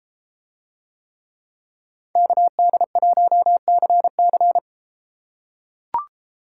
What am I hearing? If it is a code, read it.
Morse code, heard as a keyed tone, about 31 wpm, 700 Hz. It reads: KD1CC